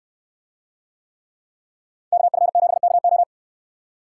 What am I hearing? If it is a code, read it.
Morse code, heard as a keyed tone, about 69 wpm, 700 Hz. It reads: KF7KO